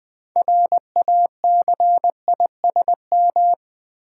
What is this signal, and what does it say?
Morse code, heard as a keyed tone, about 20 wpm, 700 Hz. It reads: RACISM
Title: 6 Letter Words - First 3 Letters Given - 20wpm